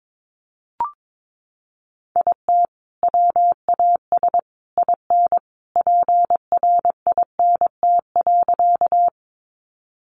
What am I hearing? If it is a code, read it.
Morse code, heard as a keyed tone, about 22 wpm, 700 Hz. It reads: IT WAS IN PRINT.
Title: Sentences from Top 1400 Words 22wpm